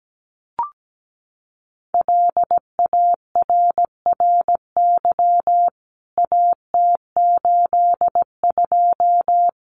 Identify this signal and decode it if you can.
Morse code, heard as a keyed tone, about 17 wpm, 700 Hz. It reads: LARRY AT82
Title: CWT Exchange 17wpm